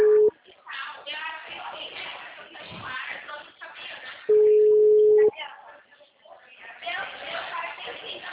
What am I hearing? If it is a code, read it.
Ringback tone (Europe, 425 Hz), with speech.